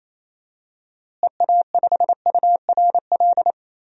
Morse code, heard as a keyed tone, about 28 wpm, 700 Hz. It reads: EA5URL
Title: DXCC-100 Call Signs - 6 Characters - Repeated 2x - 28wpm